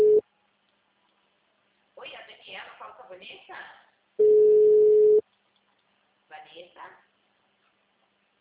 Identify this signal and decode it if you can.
Ringback tone (Europe, 425 Hz), with speech.
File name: out- -2009- - - .86004